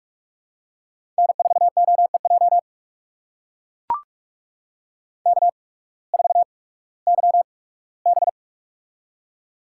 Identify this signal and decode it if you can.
Morse code, heard as a keyed tone, about 42 wpm, spaced as for 21 wpm, 700 Hz. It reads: N4OEJ K4YB
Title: USA Call Signs - 8x Character Spacing - 45wpm